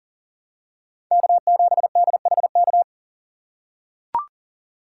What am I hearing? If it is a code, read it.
Morse code, heard as a keyed tone, about 40 wpm, 700 Hz. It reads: K7DHK
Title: USA Call Signs - 5 Characters - 6x Character Spacing - 40wpm